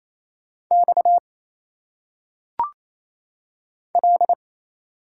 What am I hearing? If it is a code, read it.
Morse code, heard as a keyed tone, about 28 wpm, 700 Hz. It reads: X L
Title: Single Letters - Phonetic Pronunciation 28wpm